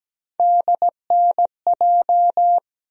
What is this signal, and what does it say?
Morse code, heard as a keyed tone, about 17 wpm, 700 Hz. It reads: DNJ
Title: USA Call Sign Postfixes - 3 Letters 17wpm